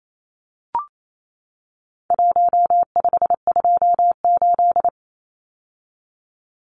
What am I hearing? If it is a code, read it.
Morse code, heard as a keyed tone, about 28 wpm, 700 Hz. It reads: 1528
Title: Numbers - Contest Style - 4-Digit 28wpm